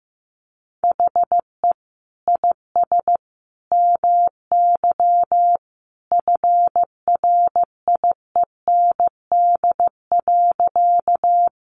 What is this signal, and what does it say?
Morse code, heard as a keyed tone, about 15 wpm, 700 Hz. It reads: HE IS MY FRIEND.
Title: Sentences from Top 300 Words 15wpm